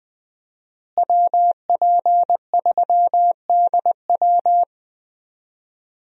Morse code, heard as a keyed tone, about 20 wpm, 700 Hz. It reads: WP3DW
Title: USA Call Signs - 5 Characters - Repeated 2x - 20wpm